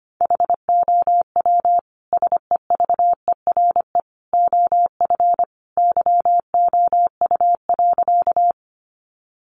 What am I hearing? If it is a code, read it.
Morse code, heard as a keyed tone, about 25 wpm, 700 Hz. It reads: HOW SEVERE OF YOU.
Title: Sentences from Top 1400 Words 25wpm